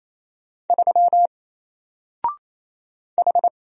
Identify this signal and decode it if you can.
Morse code, heard as a keyed tone, about 27 wpm, 700 Hz. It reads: 3 H
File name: Single-Letter-Number-28wpm